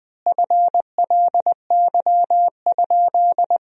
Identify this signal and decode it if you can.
Morse code, heard as a keyed tone, about 20 wpm, 700 Hz. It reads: FLY?
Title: 3x Word-Spacing - Sentences from Top 500 Words - 20wpm